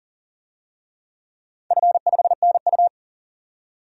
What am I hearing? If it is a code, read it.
Morse code, heard as a keyed tone, about 40 wpm, 700 Hz. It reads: F5NU
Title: DXCC-100 Call Signs - 4 Characters - 40wpm